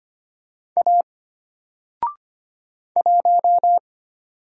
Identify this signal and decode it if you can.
Morse code, heard as a keyed tone, about 25 wpm, 700 Hz. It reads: A 1
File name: Single-Letter-Number-25wpm